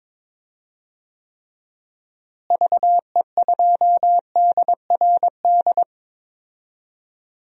Morse code, heard as a keyed tone, about 22 wpm, 700 Hz. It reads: VE2DRD
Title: DXCC-100 Call Signs - 6 Characters - Repeated 3x - 22wpm